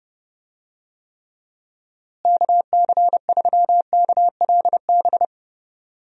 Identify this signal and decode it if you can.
Morse code, heard as a keyed tone, about 28 wpm, 700 Hz. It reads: KC3KLB
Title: USA Call Signs - 6 Characters - Repeated 3x - 30wpm